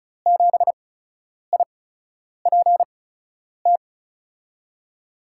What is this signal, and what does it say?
Morse code, heard as a keyed tone, about 34 wpm, spaced as for 10 wpm, 700 Hz. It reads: 7IPT
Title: USA Call Signs - 6 Characters - 8x Character Spacing - 35wpm